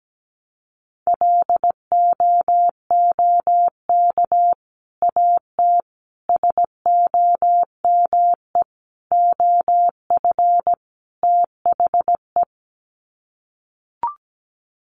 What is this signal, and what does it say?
Morse code, heard as a keyed tone, about 17 wpm, 700 Hz. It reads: LOOK AT SOME OF THE
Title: Sets of 5 Words Encore 17wpm